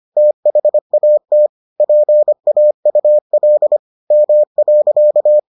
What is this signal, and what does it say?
Morse code, heard as a keyed tone, about 25 wpm, 600 Hz. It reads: THAT PAUL M.